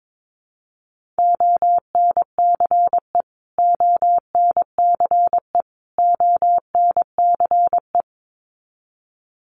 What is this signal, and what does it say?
Morse code, heard as a keyed tone, about 22 wpm, 700 Hz. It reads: ONCE ONCE ONCE
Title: Top 500 Words Repeated 22wpm